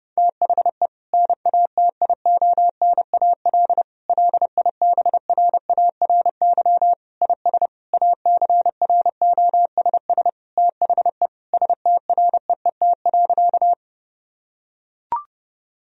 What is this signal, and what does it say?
Morse code, heard as a keyed tone, about 32 wpm, 700 Hz. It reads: THE NATIONAL LIBRARY IS ACROSS THE STREET.
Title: Sentences from Top 1000 Words - Review Part 4 - 30wpm